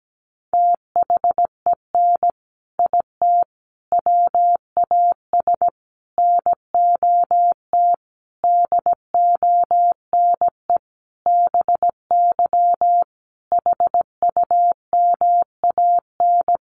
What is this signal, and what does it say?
Morse code, heard as a keyed tone, about 17 wpm, 700 Hz. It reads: THEN IT WAS NOT DONE BY HUMAN
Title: Sentences from Top 1300 Words 17wpm